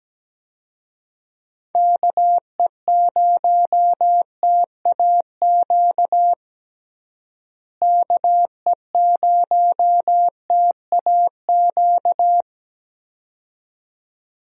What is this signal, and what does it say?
Morse code, heard as a keyed tone, about 17 wpm, 700 Hz. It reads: KE0TAQ KE0TAQ
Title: USA Call Signs - 6 Characters - Repeated 2x - 17wpm